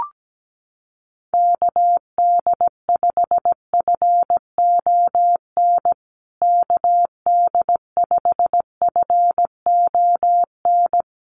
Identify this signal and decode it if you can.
Morse code, heard as a keyed tone, about 17 wpm, 700 Hz. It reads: KD5FON KD5FON
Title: USA Call Signs Repeated 3x 17wpm